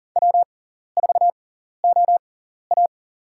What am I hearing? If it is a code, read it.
Morse code, heard as a keyed tone, about 39 wpm, spaced as for 14 wpm, 700 Hz. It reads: W4OA
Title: USA Call Signs - 4 Characters - 6x Character Spacing - 40wpm